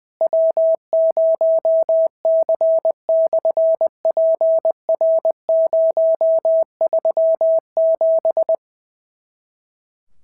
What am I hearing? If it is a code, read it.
Morse code, heard as a keyed tone, about 20 wpm, 650 Hz. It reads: W0C/PR037
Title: SOTA QSO Practice 20wpm